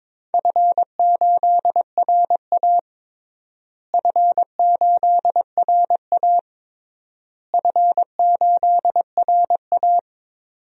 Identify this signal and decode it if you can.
Morse code, heard as a keyed tone, about 22 wpm, 700 Hz. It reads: F8RA F8RA F8RA
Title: DXCC-100 Call Signs - 4 Characters - Repeated 3x - 22wpm